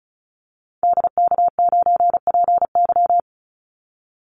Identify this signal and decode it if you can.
Morse code, heard as a keyed tone, about 34 wpm, 700 Hz. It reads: DK9PY
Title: Call Signs Repeated 2x - 35wpm